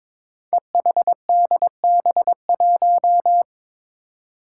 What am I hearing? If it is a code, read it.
Morse code, heard as a keyed tone, about 23 wpm, 700 Hz. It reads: EHDB1
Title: Five Challenging Letter-Number Combinations 22wpm